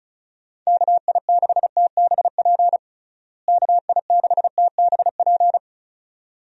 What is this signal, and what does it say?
Morse code, heard as a keyed tone, about 36 wpm, 700 Hz. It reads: KI6TBP KI6TBP
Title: USA Call Signs - 6 Characters - Repeated 2x - 35wpm